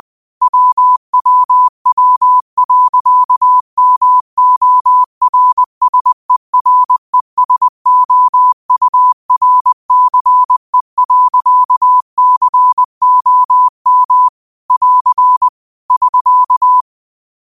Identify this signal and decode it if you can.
Morse code, heard as a keyed tone, about 20 wpm, 1000 Hz. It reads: WWW.MORSERESOURCE.COM <AR> <SK>
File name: qotdfiles / 20wpm